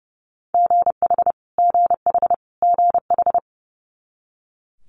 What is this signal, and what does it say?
Morse code, heard as a keyed tone, about 30 wpm, 700 Hz. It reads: GH GH GH
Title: Multi-letter Phonemes Repeated 3x 30wpm